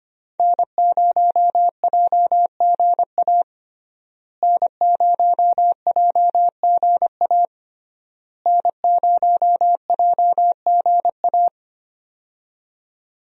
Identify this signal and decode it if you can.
Morse code, heard as a keyed tone, about 25 wpm, 700 Hz. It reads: N0JGA N0JGA N0JGA
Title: USA Call Signs - 5 Characters - Repeated 3x - 25wpm